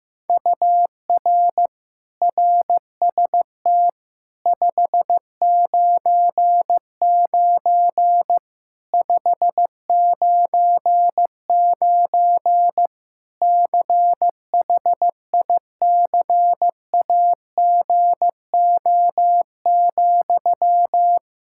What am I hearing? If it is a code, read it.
Morse code, heard as a keyed tone, about 15 wpm, 700 Hz. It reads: UR RST 599 599 CHICAGO,
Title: Practice QSO - Exchange 1 - Beginning - 15wpm